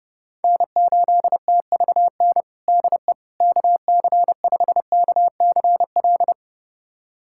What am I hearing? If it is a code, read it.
Morse code, heard as a keyed tone, about 30 wpm, 700 Hz. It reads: N8TVN DE KC5KCL
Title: USA Callsign DE Callsign 30wpm